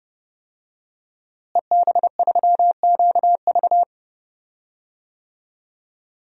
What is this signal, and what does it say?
Morse code, heard as a keyed tone, about 30 wpm, 700 Hz. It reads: EB3QV
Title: DXCC-100 Call Signs - 5 Characters - Repeated 3x - 30wpm